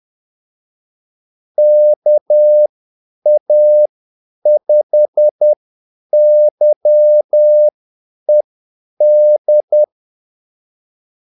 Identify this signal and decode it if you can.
Morse code, heard as a keyed tone, about 10 wpm, spaced as for 8 wpm, 600 Hz. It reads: KA5YED
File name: Callsigns-10-8wpm